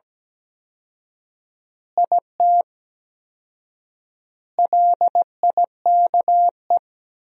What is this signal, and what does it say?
Morse code, heard as a keyed tone, about 17 wpm, 700 Hz. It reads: IT LIKE
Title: 4x Word-Spacing - Sentences from Top 300 Words - 17wpm